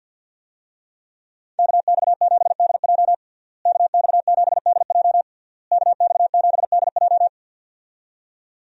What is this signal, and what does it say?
Morse code, heard as a keyed tone, about 51 wpm, 700 Hz. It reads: KX7DJ KX7DJ KX7DJ
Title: USA Call Signs - 5 Characters - Repeated 3x - 50wpm